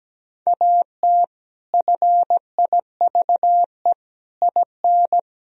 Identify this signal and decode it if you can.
Morse code, heard as a keyed tone, about 17 wpm, 700 Hz. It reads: AT FIVE IN
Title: Sentences from Top 1000 Words - Review Part 2 - 17wpm